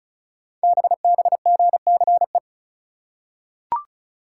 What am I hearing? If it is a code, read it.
Morse code, heard as a keyed tone, about 34 wpm, 700 Hz. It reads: BBGCE
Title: Code Groups - 5 Letter:Numbers - 35wpm